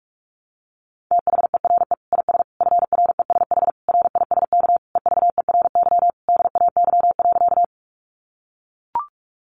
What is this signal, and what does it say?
Morse code, heard as a keyed tone, about 44 wpm, 700 Hz. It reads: THERE IS FRESH RISK EVERY DAY.
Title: Sentences from Top 1200 Words 45wpm